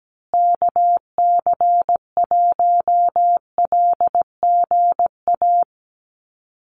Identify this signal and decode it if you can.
Morse code, heard as a keyed tone, about 17 wpm, 700 Hz. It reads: KC1LGA
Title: USA Call Signs Repeated 3x 17wpm